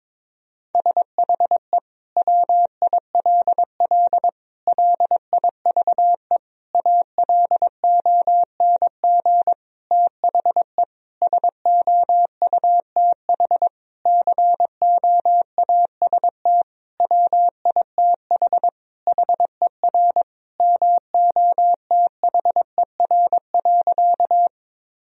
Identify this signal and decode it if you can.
Morse code, heard as a keyed tone, about 22 wpm, 700 Hz. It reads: SHE WILL LIVE ALONG THE SOUTH COAST WITH HER MOTHER.
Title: Sentences from Top 1000 Words - Review Part 4 - 22wpm